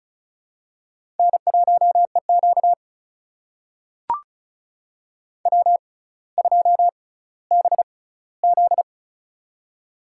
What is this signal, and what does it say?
Morse code, heard as a keyed tone, about 35 wpm, spaced as for 18 wpm, 700 Hz. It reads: N1EQ W2BZ
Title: USA Call Signs - 4 Characters - 6x Character Spacing - 35wpm